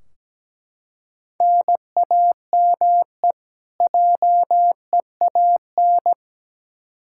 Morse code, heard as a keyed tone, about 17 wpm, 700 Hz. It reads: NAME JEAN
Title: Practice QSO - Exchange 1 - Beginning - 17wpm